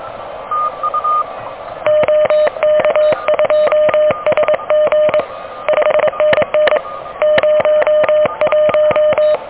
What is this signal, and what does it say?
Morse code, heard as a keyed tone, about 23 wpm, 610 Hz. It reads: OK2SG 5NN 01
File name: OK QRP 2023 / OK QRP 2023